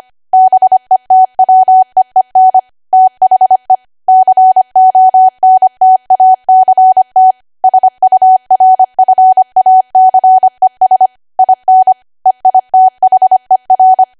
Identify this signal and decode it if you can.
Morse code, heard as a keyed tone, about 25 wpm, 750 Hz. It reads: BETWEEN THE CONTACT SURFACES IN EITHER
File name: Archive / 25 WPM / 25WPM